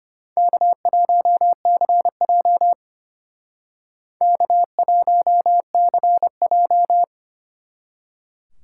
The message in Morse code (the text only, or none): K1CJ K1CJ